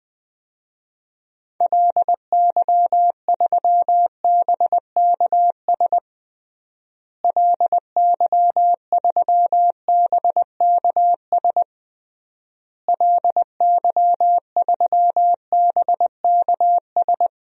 Morse code, heard as a keyed tone, about 21 wpm, 700 Hz. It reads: LY3BKS LY3BKS LY3BKS
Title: DXCC-100 Call Signs - 6 Characters - Repeated 3x - 20wpm